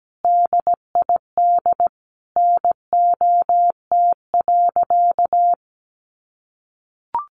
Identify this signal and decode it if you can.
Morse code, heard as a keyed tone, about 17 wpm, 700 Hz. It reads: DID NOT.
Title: Sentences from Top 600 Words 17wpm